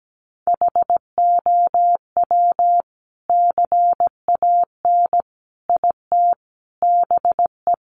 Morse code, heard as a keyed tone, about 17 wpm, 700 Hz. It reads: HOW CAN IT BE
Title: Sentences from Top 1200 Words 17wpm